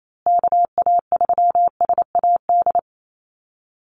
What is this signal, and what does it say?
Morse code, heard as a keyed tone, about 28 wpm, 700 Hz. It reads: KA3SAD
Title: USA Call Signs Repeated 2x 28wpm